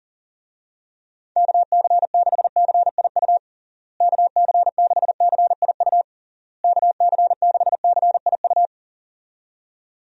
Morse code, heard as a keyed tone, about 40 wpm, 700 Hz. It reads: KC6CIU KC6CIU KC6CIU
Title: USA Call Signs - 6 Characters - Repeated 3x - 40wpm